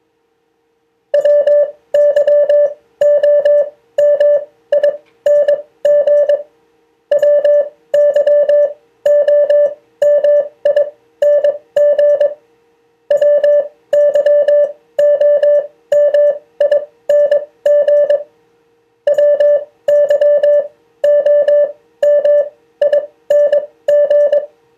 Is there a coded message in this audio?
Yes: Morse code from 1.1 to 24.4 s, heard as a keyed tone, about 22 wpm, spaced as for 16 wpm, 580 Hz. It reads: WYOMING WYOMING WYOMING WYOMING